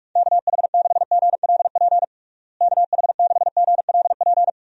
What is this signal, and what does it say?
Morse code, heard as a keyed tone, about 42 wpm, 690 Hz. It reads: KH6GLP KH6GLP
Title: DXCC-100 Call Signs - 6 Characters - Repeated 2x - 45wpm